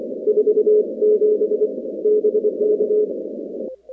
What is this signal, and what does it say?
Morse code, heard as a keyed tone, about 25 wpm, 420 Hz. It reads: 47 BK
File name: contests / 2019-cw-wpx-kt5j-k5pi- -2330 0-19